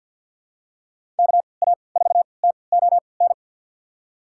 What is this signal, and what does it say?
Morse code, heard as a keyed tone, about 51 wpm, spaced as for 28 wpm, 690 Hz. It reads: KA4TON
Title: USA Call Signs - 6 Characters - 3x Character Spacing - 50wpm